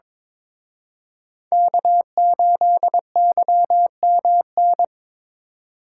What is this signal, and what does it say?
Morse code, heard as a keyed tone, about 27 wpm, 700 Hz. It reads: K8YMN
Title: Speed Racing - USA 1x3 Call Signs 22wpm